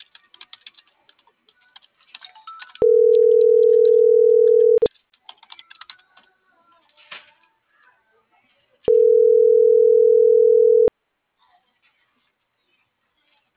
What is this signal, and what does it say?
Ringback tone (North America), 440 + 480 Hz.
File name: CENTRAL NMP / out itzel.arias